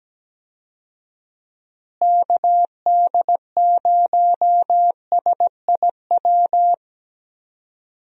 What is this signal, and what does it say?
Morse code, heard as a keyed tone, about 17 wpm, 700 Hz. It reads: KD0SIW
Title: USA Call Signs - 6 Characters - Repeated 2x - 17wpm